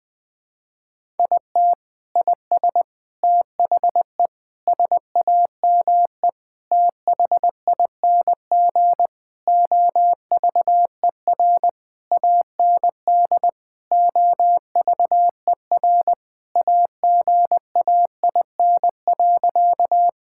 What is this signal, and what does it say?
Morse code, heard as a keyed tone, about 21 wpm, 700 Hz. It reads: IT IS THE SAME THING OVER AND OVER AGAIN.
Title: Sentences with 6-Word Phrases - 20wpm